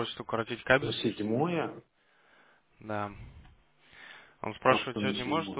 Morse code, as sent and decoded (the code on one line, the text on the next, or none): none